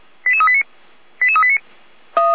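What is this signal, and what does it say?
Five-tone selective call (ZVEI), heard as quick runs of five tones, tones of about 80 ms. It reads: 88389 88389
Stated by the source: ZVEI five-tone selective call received by SDR.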